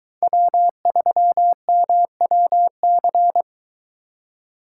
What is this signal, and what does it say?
Morse code, heard as a keyed tone, about 22 wpm, 700 Hz. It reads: W3MWC